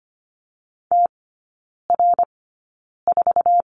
Morse code, heard as a keyed tone, about 25 wpm, spaced as for 9 wpm, 700 Hz. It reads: TR4
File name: course / 25:10wpm / 093 Lesson 16e - Letter c - Three Characters 25wpm